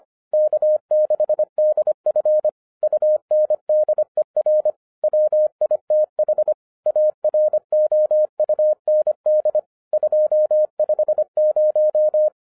Morse code, heard as a keyed tone, about 24 wpm, 610 Hz. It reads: K6DF UNDER WITH AROUND 250
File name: 2012-01-mp3 / n14-FUN WITH NUMBERS-25wpm